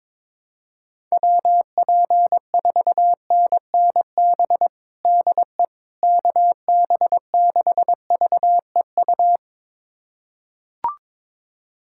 Morse code, heard as a keyed tone, about 21 wpm, 700 Hz. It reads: WP4NNB DE KB6VEU
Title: USA Callsign DE Callsign - 6 Characters - 22wpm